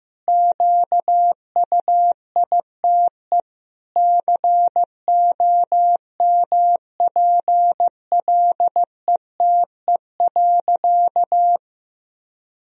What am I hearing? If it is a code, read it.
Morse code, heard as a keyed tone, about 15 wpm, 700 Hz. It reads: QUITE COMPLETE.
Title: Sentences from Top 1000 Words - Review Part 2 - 15wpm